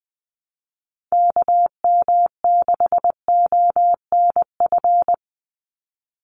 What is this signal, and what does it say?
Morse code, heard as a keyed tone, about 20 wpm, 700 Hz. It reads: KM6ONF
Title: USA Call Signs 20wpm